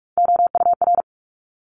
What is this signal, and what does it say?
Morse code, heard as a keyed tone, about 44 wpm, 700 Hz. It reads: OUR
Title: Top 200 Words Repeated 45wpm